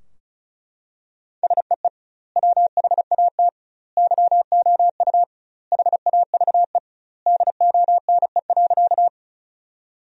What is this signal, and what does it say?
Morse code, heard as a keyed tone, about 34 wpm, 700 Hz. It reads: SEE WHAT YOU HAVE DONE.
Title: 2x Word-Spacing - Sentences from Top 500 Words - 35wpm